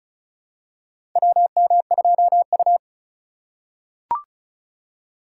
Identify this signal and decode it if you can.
Morse code, heard as a keyed tone, about 34 wpm, 700 Hz. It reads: WM2U
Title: USA Call Signs - 4 Characters - 2x Character Spacing - 35wpm